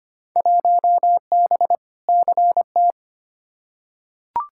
Morse code, heard as a keyed tone, about 25 wpm, 700 Hz. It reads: 1B CT
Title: Field Day Exchanges Encore - 25wpm